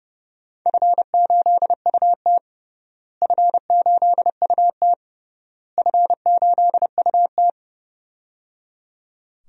Morse code, heard as a keyed tone, about 30 wpm, 700 Hz. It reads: F8UT F8UT F8UT
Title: DXCC-100 Call Signs - 4 Characters - Repeated 3x - 30wpm